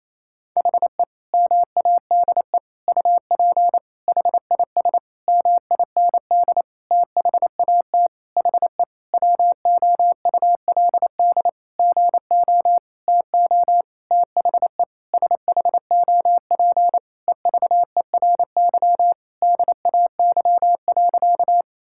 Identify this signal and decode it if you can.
Morse code, heard as a keyed tone, about 28 wpm, 700 Hz. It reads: HE MADE UP HIS MIND THAT HE WOULD GO TO THE SHOP EVERY DAY.
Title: Sentences from Top 1600 Words 28wpm